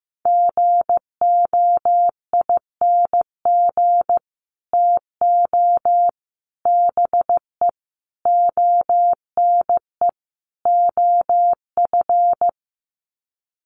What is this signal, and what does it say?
Morse code, heard as a keyed tone, about 15 wpm, 700 Hz. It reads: GOING TO BE ONE OF
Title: Sets of 5 Words Encore2 15wpm